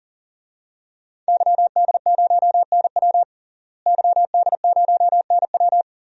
Morse code, heard as a keyed tone, about 40 wpm, 700 Hz. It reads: YD0NW YD0NW
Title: DXCC-100 Call Signs - 5 Characters - Repeated 3x - 40wpm